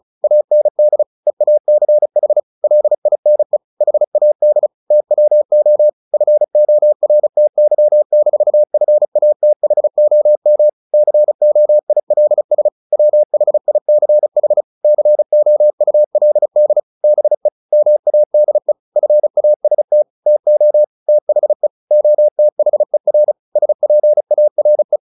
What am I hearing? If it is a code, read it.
Morse code, heard as a keyed tone, about 34 wpm, 600 Hz. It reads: AND EACH LINE HAD TWO FORTY-FATHOM COILS WHICH COULD BE MADE FAST TO THE OTHER SPARE
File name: the old man and the sea / the old man and the sea 1 35wpm